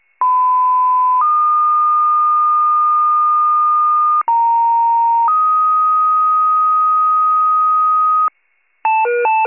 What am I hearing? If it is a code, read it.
Two-tone page, heard as a one-second tone, then a three-second tone, about 1010 then 1220 Hz, about 930 then 1250 Hz. It is heard from 0.2 to 8.3 s.